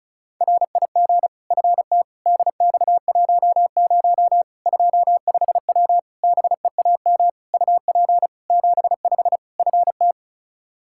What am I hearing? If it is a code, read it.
Morse code, heard as a keyed tone, about 36 wpm, 700 Hz. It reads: RIG FT DX10 25W BEAM UP 75 FT
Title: Practice QSO - Exchange 2 - Info - 35wpm